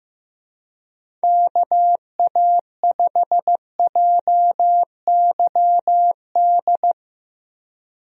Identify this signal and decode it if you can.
Morse code, heard as a keyed tone, about 15 wpm, 700 Hz. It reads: KA5JYD
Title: USA Call Signs - 6 Characters - Repeated 2x - 15wpm